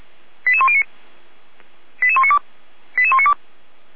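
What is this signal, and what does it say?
Five-tone selective call (ZVEI), heard as quick runs of five tones, tones of about 70 ms. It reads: 88109 88192 88192
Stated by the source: ZVEI five-tone selective call received by SDR, archived by Feuerwehr Orlamünde (fire brigade).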